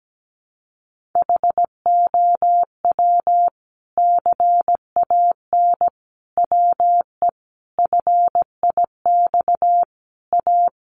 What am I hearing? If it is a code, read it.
Morse code, heard as a keyed tone, about 17 wpm, 700 Hz. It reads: HOW CAN WE FIX A
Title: Sentences from Top 1400 Words 17wpm